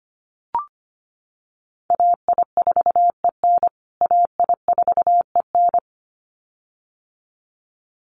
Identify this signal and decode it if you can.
Morse code, heard as a keyed tone, about 25 wpm, 700 Hz. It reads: AI4EN AI4EN
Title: USA Call Signs Repeated 2x 25wpm